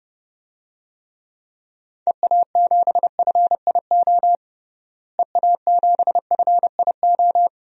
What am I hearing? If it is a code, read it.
Morse code, heard as a keyed tone, about 30 wpm, 700 Hz. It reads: EA7FIO EA7FIO
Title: DXCC-100 Call Signs - 6 Characters - Repeated 2x - 30wpm